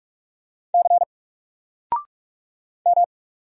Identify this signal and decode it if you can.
Morse code, heard as a keyed tone, about 46 wpm, 700 Hz. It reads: C M
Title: Single Letter-Number 45wpm